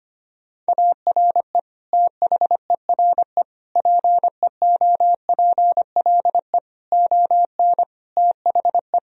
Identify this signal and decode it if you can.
Morse code, heard as a keyed tone, about 25 wpm, 700 Hz. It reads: ARE THERE PEOPLE ON THE
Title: Sentences from Top 1000 Words - Review Part 3 - 25wpm